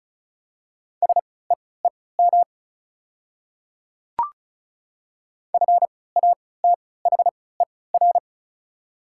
Morse code, heard as a keyed tone, about 36 wpm, spaced as for 20 wpm, 700 Hz. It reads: SEEM FATHER
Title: 3x Character Spacing - Top 300 Words - 35wpm